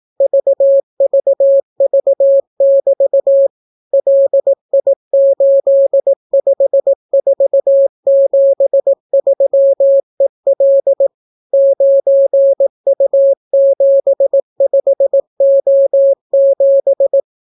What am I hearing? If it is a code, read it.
Morse code, heard as a keyed tone, about 18 wpm, 550 Hz. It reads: VVV<BT> LI85473EL 9U75O7